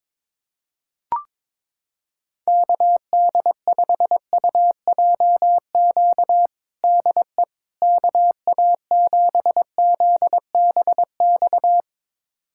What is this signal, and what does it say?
Morse code, heard as a keyed tone, about 22 wpm, 700 Hz. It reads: KD5UJQ DE KA7ZBX
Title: USA Callsign DE Callsign - 6 Characters - 22wpm